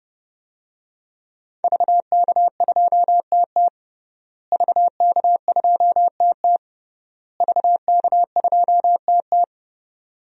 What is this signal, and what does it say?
Morse code, heard as a keyed tone, about 30 wpm, 700 Hz. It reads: VK2TT VK2TT VK2TT
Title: DXCC-100 Call Signs - 5 Characters - Repeated 3x - 30wpm